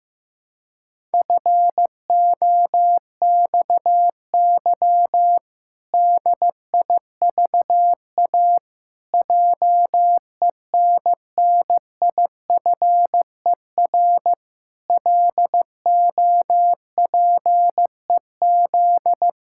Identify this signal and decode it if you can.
Morse code, heard as a keyed tone, about 16 wpm, 700 Hz. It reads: FOXY DIVA JENNIFER LOPEZ
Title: Pangram Challenge v2 - 15wpm